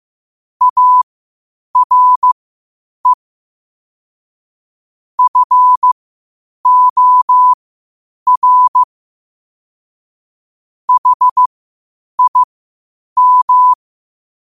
Morse code, heard as a keyed tone, about 15 wpm, spaced as for 8 wpm, 1000 Hz. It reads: ARE FOR HIM
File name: 8wpm